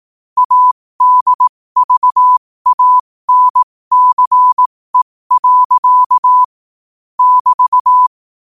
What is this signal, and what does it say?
Morse code, heard as a keyed tone, about 18 wpm, spaced as for 15 wpm, 1000 Hz. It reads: ADVANCE. <BT>
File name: qotdfiles / 15wpm